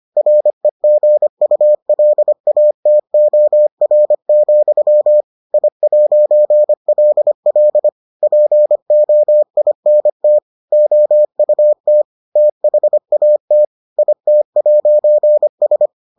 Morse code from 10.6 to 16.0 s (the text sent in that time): OUT THAT IT'S